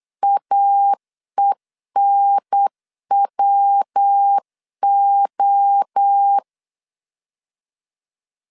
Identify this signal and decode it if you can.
Morse code, heard as a keyed tone, about 8 wpm, 800 Hz. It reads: AENWO
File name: CW / DET / 1 3 1500C S B 8wpm 4141